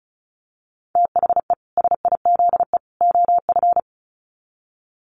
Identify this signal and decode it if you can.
Morse code, heard as a keyed tone, about 34 wpm, 700 Hz. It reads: THE SIZE OF
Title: Sets of 3 Words 35wpm